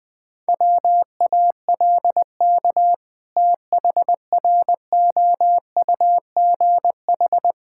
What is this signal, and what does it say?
Morse code, heard as a keyed tone, about 21 wpm, 700 Hz. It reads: WALK THROUGH
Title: Sentences with 2-Word Phrases - Spoken Beginning - 20wpm